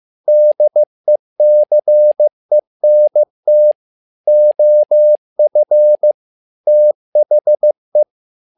Morse code, heard as a keyed tone, about 15 wpm, 600 Hz. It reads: DECENT OF THE
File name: learning center / tower trouble / tower trouble 15wpm